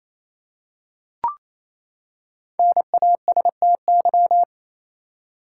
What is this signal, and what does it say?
Morse code, heard as a keyed tone, about 28 wpm, 700 Hz. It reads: NASTY
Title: Five Letter Words - 28wpm